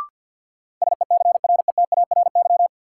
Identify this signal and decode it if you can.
Morse code, heard as a keyed tone, about 53 wpm, 700 Hz. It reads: SECRETARY
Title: Sentences from Top 1300 Words - New Word Drill - 50wpm